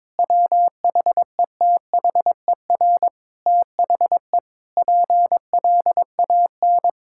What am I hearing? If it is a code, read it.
Morse code, heard as a keyed tone, about 22 wpm, 700 Hz. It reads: WHETHER THE PLAN
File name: Sentences with 7-Word Phrases - 22wpm